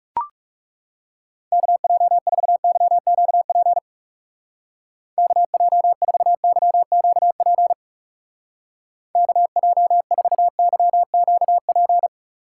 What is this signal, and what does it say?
Morse code, heard as a keyed tone, about 37 wpm, 700 Hz. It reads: KJ4YQP KJ4YQP KJ4YQP
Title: Speed Racing - USA 2x3 Call Signs 30wpm